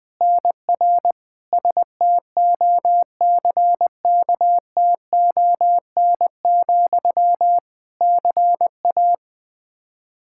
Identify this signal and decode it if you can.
Morse code, heard as a keyed tone, about 21 wpm, 700 Hz. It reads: NR STOCKTON, CA
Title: Practice QSO - Exchange 1 - Beginning - 20wpm